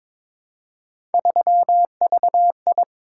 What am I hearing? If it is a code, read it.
Morse code, heard as a keyed tone, about 22 wpm, 700 Hz. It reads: 3VI
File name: practice / 22wpm / Three Letter-Number Combinations - 22wpm